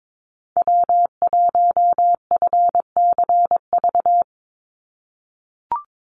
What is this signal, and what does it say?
Morse code, heard as a keyed tone, about 22 wpm, 700 Hz. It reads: W1FCV
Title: Call Signs - 22wpm